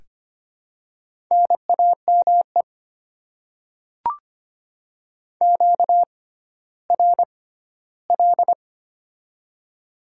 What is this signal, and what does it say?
Morse code, heard as a keyed tone, about 25 wpm, spaced as for 14 wpm, 700 Hz. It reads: NAME QRL
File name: Top 20 QSO Elements - 6x Character Spacing - 25wpm